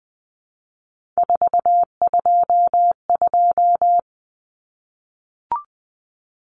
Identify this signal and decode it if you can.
Morse code, heard as a keyed tone, about 20 wpm, 700 Hz. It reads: 422
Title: Common Cut Numbers - 3-Digit 20wpm